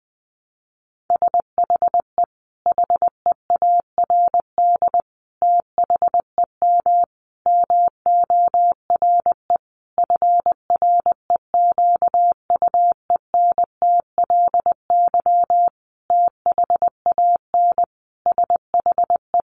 Morse code, heard as a keyed tone, about 20 wpm, 700 Hz. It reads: SHE HEARD THEM MORE FREQUENTLY THAN SHE
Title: Sentences from Top 1300 Words 20wpm